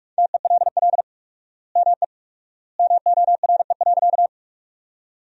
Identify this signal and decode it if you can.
Morse code, heard as a keyed tone, about 46 wpm, 700 Hz. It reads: TELL ME MORE.
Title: 4x Word-Spacing - Sentences from Top 300 Words - 45wpm